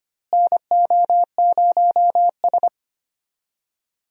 Morse code, heard as a keyed tone, about 25 wpm, 700 Hz. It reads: NO0S
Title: USA Call Signs - 4 Characters - Repeated 3x - 25wpm